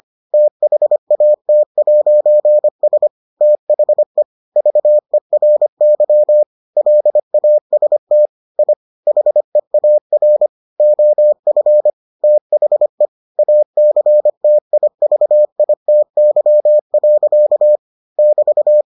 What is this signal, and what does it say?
Morse code, heard as a keyed tone, about 25 wpm, 600 Hz. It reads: THAT'S THE VERY LAST I HEAR OF THE ACTIVITY. <BT>